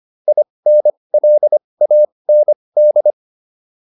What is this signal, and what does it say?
Morse code, heard as a keyed tone, about 25 wpm, spaced as for 20 wpm, 600 Hz. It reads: INLAND